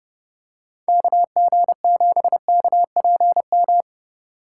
Morse code, heard as a keyed tone, about 30 wpm, 700 Hz. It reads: KG7KPM
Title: USA Call Signs - 6 Characters - 4x Character Spacing - 30wpm